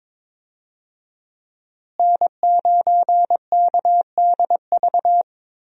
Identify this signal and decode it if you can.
Morse code, heard as a keyed tone, about 22 wpm, 700 Hz. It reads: N9KDV
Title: USA Call Signs - 5 Characters - Repeated 2x - 22wpm